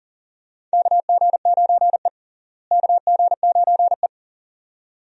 Morse code, heard as a keyed tone, about 37 wpm, 700 Hz. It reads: KG9E KG9E
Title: USA Call Signs - 4 Characters - Repeated 2x - 40wpm